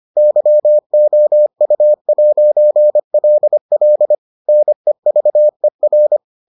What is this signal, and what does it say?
Morse code, heard as a keyed tone, about 25 wpm, 600 Hz. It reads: YOU'LL NEVER